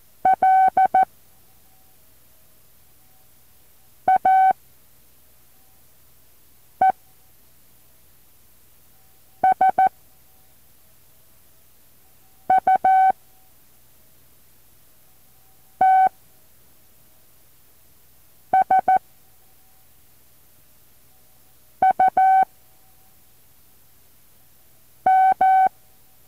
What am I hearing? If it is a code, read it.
Morse code, heard as a keyed tone, about 14 wpm, 760 Hz. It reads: L A E S U T S U M
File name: intl morse code2